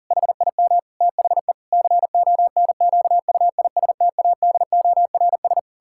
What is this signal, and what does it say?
Morse code, heard as a keyed tone, about 40 wpm, 700 Hz. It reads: HIM THE CONQUISTADORS
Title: Pangram Challenge v2 - 40wpm